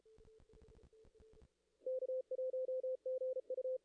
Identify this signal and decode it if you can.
Morse code, heard as a keyed tone, about 36 wpm, 520 Hz. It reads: K1GU
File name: contests / ss-cw-k5tr-op-n5rz -0000